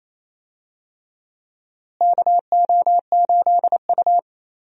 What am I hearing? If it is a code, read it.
Morse code, heard as a keyed tone, about 28 wpm, 700 Hz. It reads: KO8U
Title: USA Call Signs - 4 Characters - Repeated 3x - 28wpm